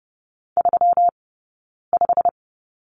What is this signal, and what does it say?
Morse code, heard as a keyed tone, about 30 wpm, 700 Hz. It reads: 3 5
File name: Lesson 20d - Number 3 - Two Characters 30wpm